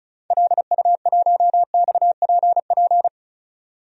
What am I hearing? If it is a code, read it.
Morse code, heard as a keyed tone, about 35 wpm, 700 Hz. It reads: LU1XPP
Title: DXCC-100 Call Signs - 6 Characters - 35wpm